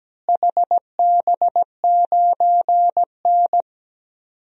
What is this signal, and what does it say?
Morse code, heard as a keyed tone, about 17 wpm, 700 Hz. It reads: HB9N
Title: DXCC-100 Call Signs - 4 Characters - Repeated 3x - 17wpm